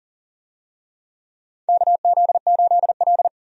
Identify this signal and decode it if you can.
Morse code, heard as a keyed tone, about 40 wpm, 700 Hz. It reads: KZ8L
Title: USA Call Signs - 4 Characters - 40wpm